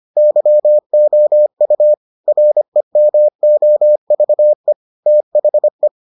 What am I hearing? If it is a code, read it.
Morse code, heard as a keyed tone, about 25 wpm, 600 Hz. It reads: YOU REMOVE THE